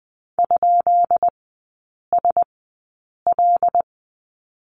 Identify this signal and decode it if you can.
Morse code, heard as a keyed tone, about 20 wpm, spaced as for 8 wpm, 690 Hz. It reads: ?SL
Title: 133 Lesson 22e - Question Mark - Three Characters 20wpm